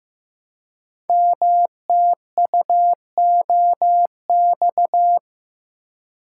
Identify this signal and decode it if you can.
Morse code, heard as a keyed tone, about 15 wpm, 700 Hz. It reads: MTUOX
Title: Code Groups - 5 Letters - 15wpm